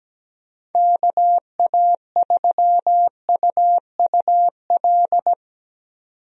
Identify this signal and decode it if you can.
Morse code, heard as a keyed tone, about 17 wpm, 700 Hz. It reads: KA3UUL
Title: USA Call Signs - 6 Characters - 17wpm